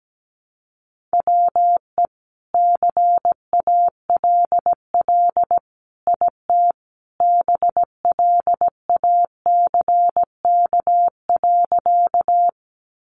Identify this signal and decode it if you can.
Morse code, heard as a keyed tone, about 17 wpm, 700 Hz. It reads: WE CALL IT BLACK.
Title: Sentences from Top 400 Words 17wpm